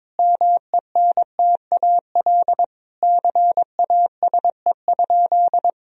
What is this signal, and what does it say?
Morse code, heard as a keyed tone, about 22 wpm, 700 Hz. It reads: MENTAL CASE?
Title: Sentences from Top 1000 Words - Review Part 4 - 22wpm